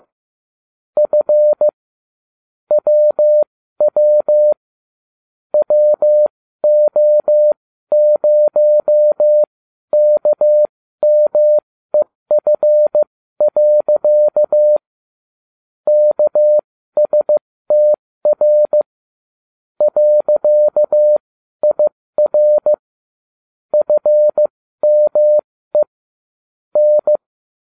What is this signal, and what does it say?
Morse code, heard as a keyed tone, about 15 wpm, spaced as for 12 wpm, 600 Hz. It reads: F WW WO0KMEF. KSTR .IR FME N